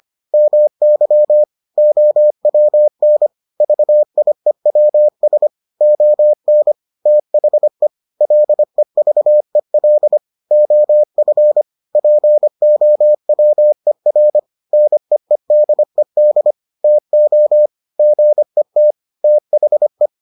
Morse code, heard as a keyed tone, about 25 wpm, 600 Hz. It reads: MY OWN VIEWS ON THE LEVEL OF POWER NEEDED TO GET THE